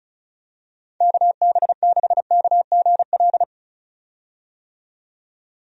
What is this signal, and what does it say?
Morse code, heard as a keyed tone, about 34 wpm, 700 Hz. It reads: KB6KGL
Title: USA Call Signs - 6 Characters - 35wpm